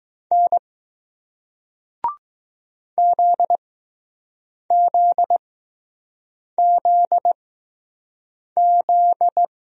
Morse code, heard as a keyed tone, about 20 wpm, spaced as for 6 wpm, 700 Hz. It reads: N ZZZZ